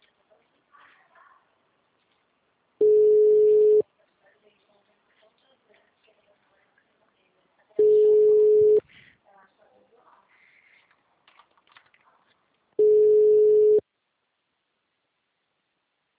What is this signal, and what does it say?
Ringback tone (Europe, 425 Hz), with speech.
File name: out- -2009- - - .4